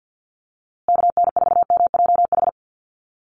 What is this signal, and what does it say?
Morse code, heard as a keyed tone, about 50 wpm, 700 Hz. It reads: KN4MJH
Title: USA Call Signs Repeated 2x 50wpm